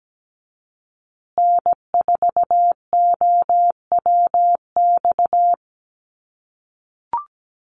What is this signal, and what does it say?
Morse code, heard as a keyed tone, about 17 wpm, 700 Hz. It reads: N4OWX
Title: USA Call Signs Repeated 3x 17wpm